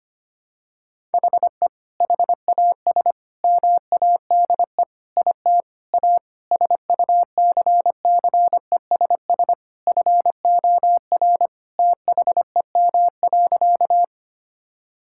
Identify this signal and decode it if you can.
Morse code, heard as a keyed tone, about 25 wpm, 700 Hz. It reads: HE HAS MADE IT A SUCCESS FOR THEM.
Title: Sentences from Top 1000 Words - Review Part 6 - 25wpm